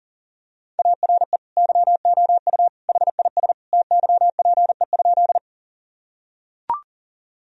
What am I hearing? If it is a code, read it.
Morse code, heard as a keyed tone, about 37 wpm, 700 Hz. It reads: ARE YOU HIS TYPE?
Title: Sentences from Top 1000 Words - Review Part 3 - 40wpm